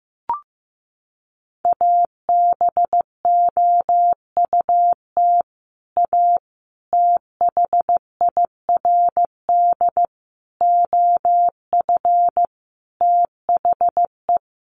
Morse code, heard as a keyed tone, about 15 wpm, 700 Hz. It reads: ABOUT A THIRD OF THE
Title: Sets of 5 Words Encore2 15wpm